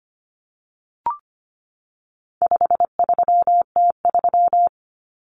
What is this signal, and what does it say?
Morse code, heard as a keyed tone, about 25 wpm, 700 Hz. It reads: 53T3
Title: Common Cut Numbers - 4-Digit 25wpm